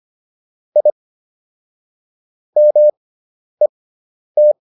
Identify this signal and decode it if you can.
Morse code, heard as a keyed tone, about 25 wpm, spaced as for 10 wpm, 600 Hz. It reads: I MET